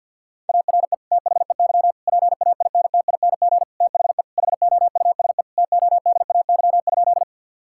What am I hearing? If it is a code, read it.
Morse code, heard as a keyed tone, about 50 wpm, 700 Hz. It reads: ARE THEY PAINTING THE HOUSE TODAY?
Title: Sentences from Top 1000 Words - Review Part 6 - 50wpm